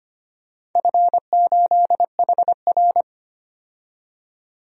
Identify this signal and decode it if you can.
Morse code, heard as a keyed tone, about 25 wpm, 700 Hz. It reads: F8HR
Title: DXCC-100 Call Signs - 4 Characters - 25wpm